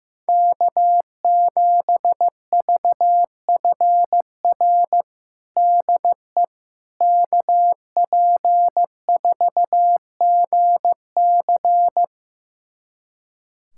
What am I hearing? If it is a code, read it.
Morse code, heard as a keyed tone, about 15 wpm, 700 Hz. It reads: K7VFR DE KP4GC
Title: USA Callsign DE Callsign - 5 Characters - 15wpm